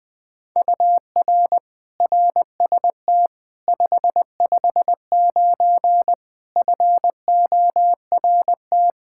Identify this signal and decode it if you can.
Morse code, heard as a keyed tone, about 20 wpm, 700 Hz. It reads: UR RST 559 FORT
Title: Practice QSO - Exchange 1 - Beginning - 20wpm